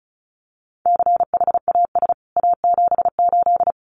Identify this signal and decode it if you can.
Morse code, heard as a keyed tone, about 34 wpm, 700 Hz. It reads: CHAS A78
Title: CWT Exchange 35wpm